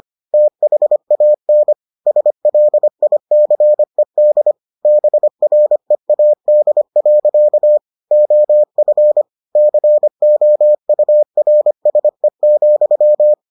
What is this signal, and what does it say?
Morse code, heard as a keyed tone, about 25 wpm, 600 Hz. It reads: THAN SLICED BREAD. OF COURSE,